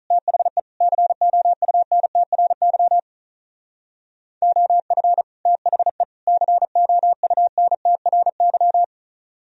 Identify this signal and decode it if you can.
Morse code, heard as a keyed tone, about 42 wpm, 700 Hz. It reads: THE COUNTRY OF THE COUNTRY
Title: Speed-Racing - Sets of 3 Words 35wpm